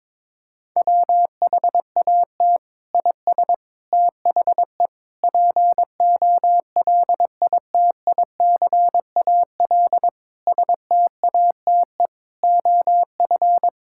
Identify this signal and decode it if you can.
Morse code, heard as a keyed tone, about 21 wpm, 700 Hz. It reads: WHAT IS THE POLITICAL STATE OF
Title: Sentences from Top 1000 Words - Review Part 5 - 22wpm